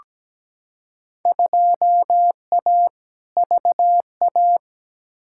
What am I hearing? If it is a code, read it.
Morse code, heard as a keyed tone, about 17 wpm, 700 Hz. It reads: 2A VA
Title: Field Day Exchanges - 17wpm